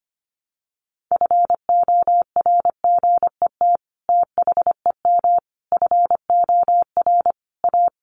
Morse code, heard as a keyed tone, about 25 wpm, 700 Hz. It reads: FORGET THEM FOR A
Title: Sentences from Top 1200 Words 25wpm